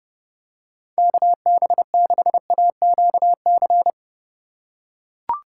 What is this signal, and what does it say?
Morse code, heard as a keyed tone, about 30 wpm, 700 Hz. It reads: KB6AQC
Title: USA Call Signs - 3x Character Spacing - 30wpm